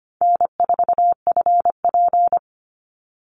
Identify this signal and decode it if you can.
Morse code, heard as a keyed tone, about 25 wpm, 700 Hz. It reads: N4FP
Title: Easy Call Signs Repeated 2x - 25wpm